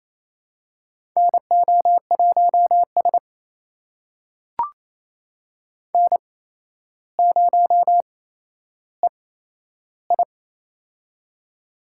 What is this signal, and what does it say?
Morse code, heard as a keyed tone, about 28 wpm, spaced as for 12 wpm, 700 Hz. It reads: NO1S N0EI
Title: USA Call Signs - 4 Characters - 8x Character Spacing - 28wpm